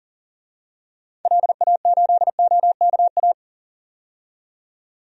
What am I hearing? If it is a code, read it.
Morse code, heard as a keyed tone, about 38 wpm, 700 Hz. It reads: LA8OKA
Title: DXCC-100 Call Signs - 6 Characters - 40wpm